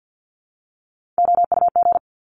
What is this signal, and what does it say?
Morse code, heard as a keyed tone, about 50 wpm, 700 Hz. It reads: OUG
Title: Top Three Letter Combinations in Words 50wpm